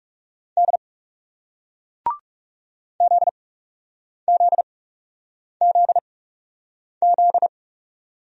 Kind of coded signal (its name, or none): Morse code